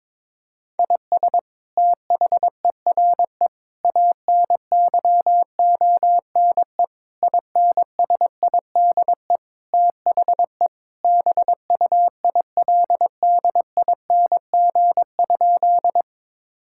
Morse code, heard as a keyed tone, about 22 wpm, 700 Hz. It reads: IS THERE ANYONE INSIDE THE BUILDING?
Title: Sentences from Top 1000 Words - Review Part 6 - 22wpm